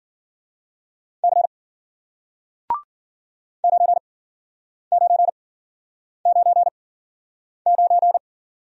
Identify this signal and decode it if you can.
Morse code, heard as a keyed tone, about 52 wpm, spaced as for 9 wpm, 700 Hz. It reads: C 9999